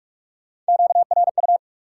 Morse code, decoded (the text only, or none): QRU